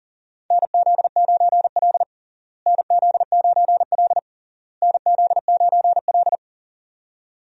Morse code, heard as a keyed tone, about 40 wpm, 700 Hz. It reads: NZ9L NZ9L NZ9L
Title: USA Call Signs - 4 Characters - Repeated 3x - 40wpm